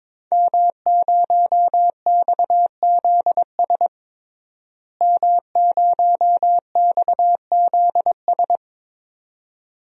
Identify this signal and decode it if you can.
Morse code, heard as a keyed tone, about 22 wpm, 700 Hz. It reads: M0XZS M0XZS
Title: DXCC-100 Call Signs - 5 Characters - Repeated 2x - 22wpm